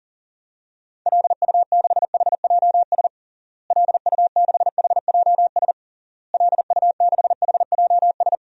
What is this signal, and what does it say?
Morse code, heard as a keyed tone, about 37 wpm, 700 Hz. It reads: LU6HJS LU6HJS LU6HJS
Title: DXCC-100 Call Signs - 6 Characters - Repeated 3x - 40wpm